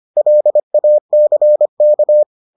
Morse code, heard as a keyed tone, about 25 wpm, 600 Hz. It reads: LACK